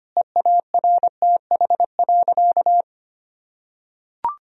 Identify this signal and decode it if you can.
Morse code, heard as a keyed tone, about 25 wpm, 700 Hz. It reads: EARTH.
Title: Sentences from Top 1000 Words - Review Part 1 - 25wpm